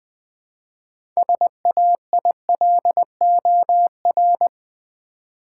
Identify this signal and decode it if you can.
Morse code, heard as a keyed tone, about 20 wpm, 700 Hz. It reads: SAILOR
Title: 6 Letter Words - First 3 Letters Given - 20wpm